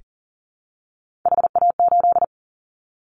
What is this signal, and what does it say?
Morse code, heard as a keyed tone, about 39 wpm, 700 Hz. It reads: HA8
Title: Callsign Prefixes for Top 50 DXCC - 40wpm